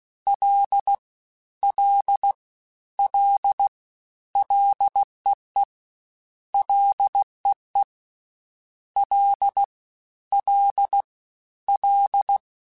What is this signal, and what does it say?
Morse code, heard as a keyed tone, about 16 wpm, 800 Hz. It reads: L L L LEE LEE L L L
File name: Ls / Single L m3s / Single L Prefixe 3x2 Twice @15